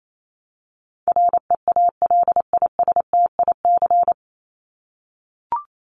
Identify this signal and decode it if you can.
Morse code, heard as a keyed tone, about 28 wpm, 700 Hz. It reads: REALISTIC
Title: Top 1751-2000 Words 28wpm